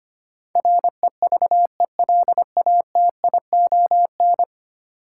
Morse code, heard as a keyed tone, about 25 wpm, 700 Hz. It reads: REVELATION
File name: Ten Letter Words - 25wpm